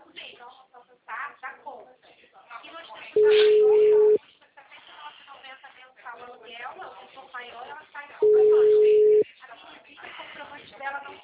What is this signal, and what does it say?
Ringback tone (Europe, 425 Hz), with speech.